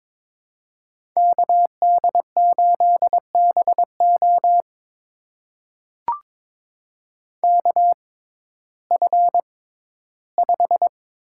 Morse code, heard as a keyed tone, about 22 wpm, spaced as for 14 wpm, 700 Hz. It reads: KD8BO KF5
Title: USA Call Signs - 5 Characters - 6x Character Spacing - 22wpm